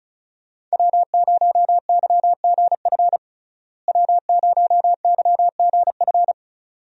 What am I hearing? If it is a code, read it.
Morse code, heard as a keyed tone, about 36 wpm, 700 Hz. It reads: W0YGF W0YGF
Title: USA Call Signs - 5 Characters - Repeated 2x - 35wpm